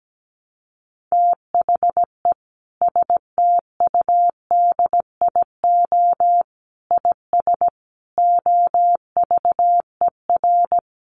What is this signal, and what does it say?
Morse code, heard as a keyed tone, about 17 wpm, 700 Hz. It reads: THE STUDIO IS OVER
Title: Sentences from Top 1200 Words 17wpm